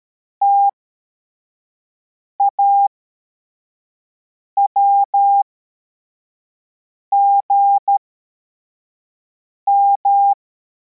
Morse code, heard as a keyed tone, about 13 wpm, spaced as for 4 wpm, 800 Hz. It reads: TAWGM
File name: formation / cw / Text64